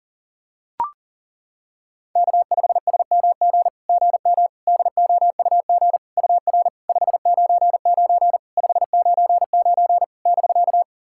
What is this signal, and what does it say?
Morse code, heard as a keyed tone, about 40 wpm, 700 Hz. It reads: K5SMG GM DOUG UR 599 599 <BK>
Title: SOTA QSO Practice 40wpm